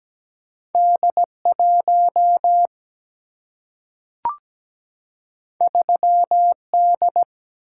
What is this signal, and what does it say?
Morse code, heard as a keyed tone, about 17 wpm, 700 Hz. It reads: D1 3D
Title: Single-Letter Single-Number 17wpm